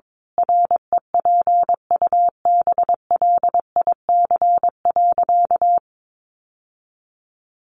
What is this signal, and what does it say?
Morse code, heard as a keyed tone, about 22 wpm, 700 Hz. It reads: REPUBLIC.
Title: Sentences from Top 1500 Words 22wpm